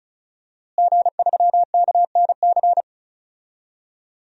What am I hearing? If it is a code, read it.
Morse code, heard as a keyed tone, about 34 wpm, 700 Hz. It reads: G3KNC
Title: DXCC-100 Call Signs - 5 Characters - Repeated 3x - 35wpm